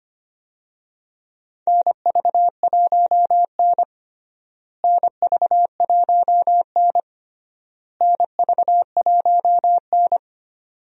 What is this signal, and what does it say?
Morse code, heard as a keyed tone, about 25 wpm, 700 Hz. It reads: NV1N NV1N NV1N
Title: USA Call Signs - 4 Characters - Repeated 3x - 25wpm